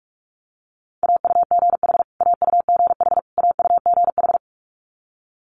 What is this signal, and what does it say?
Morse code, heard as a keyed tone, about 46 wpm, 690 Hz. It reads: AUGH AUGH AUGH
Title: Multi-letter Phonemes Repeated 3x 45wpm